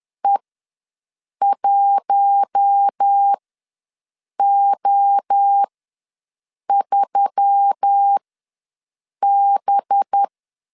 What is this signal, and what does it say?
Morse code, heard as a keyed tone, about 10 wpm, spaced as for 6 wpm, 800 Hz. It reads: E1O3B
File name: LEZIONI / 5 LBVC / 5 LBV ws 10wpm 3645